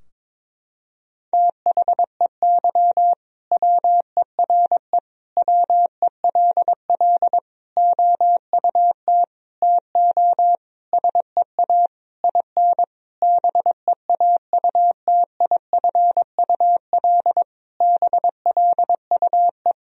Morse code, heard as a keyed tone, about 22 wpm, 700 Hz. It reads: THEY WERE WELL OUT TO SEA IN BEAUTIFUL BLUE
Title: Sentences from Top 1000 Words - Review Part 3 - 22wpm